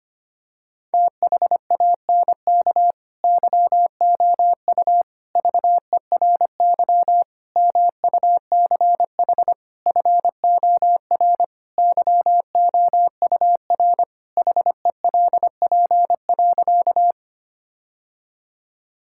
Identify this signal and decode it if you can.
Morse code, heard as a keyed tone, about 25 wpm, 700 Hz. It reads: THANK YOU VERY MUCH FOR YOUR HELP.
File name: practice / 25wpm / Sentences with 4-Word Phrases - 25wpm